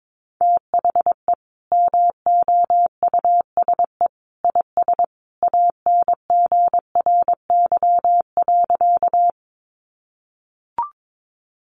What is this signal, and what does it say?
Morse code, heard as a keyed tone, about 22 wpm, 700 Hz. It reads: THE MOUSE IS ANGRY.
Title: Easy Phrases 22wpm